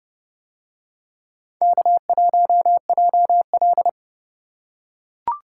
Morse code, heard as a keyed tone, about 30 wpm, 700 Hz. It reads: K1JL